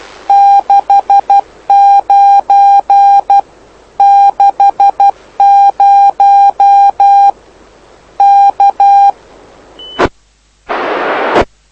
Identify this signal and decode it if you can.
Morse code, heard as a keyed tone, about 12 wpm, 800 Hz. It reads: 69 60 K
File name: lu4aao / lu7aa 09dic2011 exp planeador vuelo1